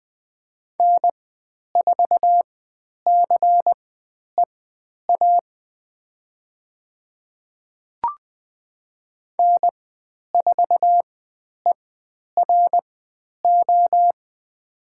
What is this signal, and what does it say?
Morse code, heard as a keyed tone, about 20 wpm, spaced as for 10 wpm, 700 Hz. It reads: N4CEA N4ERO
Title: 102 Lesson 17g - Letter u - Call Signs 20wpm